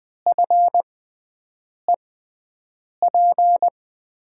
Morse code, heard as a keyed tone, about 20 wpm, spaced as for 7 wpm, 700 Hz. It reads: FEP
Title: USA Call Signs - 6x Character Spacing - 20wpm